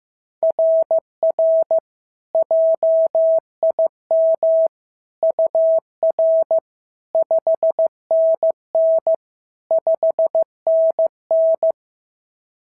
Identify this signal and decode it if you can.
Morse code, heard as a keyed tone, about 16 wpm, 690 Hz. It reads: RR JIM UR 5NN 5NN
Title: SOTA QSO Practice 15wpm